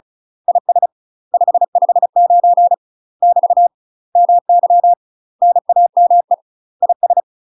Morse code, heard as a keyed tone, about 35 wpm, 700 Hz. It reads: IS 559 <BT> MY NAME IS
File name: UK MASTER QSOs TRAINING / 35WPM / UK MASTER QSOs 0003